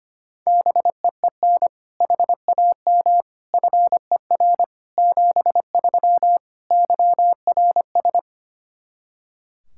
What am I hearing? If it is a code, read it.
Morse code, heard as a keyed tone, about 25 wpm, 700 Hz. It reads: BEEN HAM FER 73 YRS
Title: K1USN SST QSO Practice 25wpm